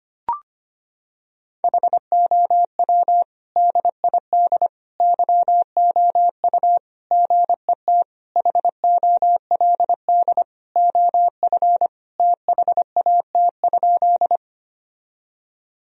Morse code, heard as a keyed tone, about 25 wpm, 700 Hz. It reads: HOW DID YOU GET HOLD OF THAT?
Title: Sentences from Top 1000 Words - Review Part 2 - 25wpm